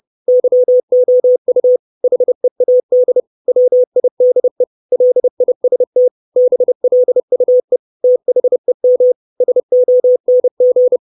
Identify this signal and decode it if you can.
Morse code, heard as a keyed tone, about 30 wpm, 500 Hz. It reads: YOU HEAD WIDE LIST BLUE THEM SONG
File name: 632 Words 30WPM